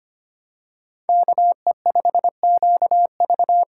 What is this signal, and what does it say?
Morse code, heard as a keyed tone, about 25 wpm, 700 Hz. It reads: KE5QV
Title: USA Call Signs - 6 Characters - 6x Character Spacing - 25wpm